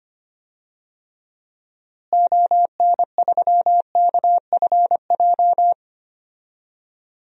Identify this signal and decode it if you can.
Morse code, heard as a keyed tone, about 25 wpm, 700 Hz. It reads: ON3KFJ
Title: DXCC-100 Call Signs - 6 Characters - Repeated 3x - 25wpm